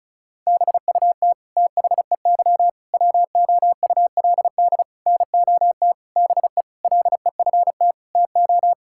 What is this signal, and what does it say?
Morse code, heard as a keyed tone, about 36 wpm, 700 Hz. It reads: BUT THEY WOULD NOT BE LEFT TO
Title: Sentences from Top 1000 Words - Review Part 5 - 35wpm